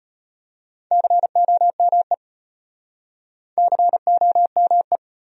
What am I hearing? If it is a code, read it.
Morse code, heard as a keyed tone, about 34 wpm, 700 Hz. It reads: COME COME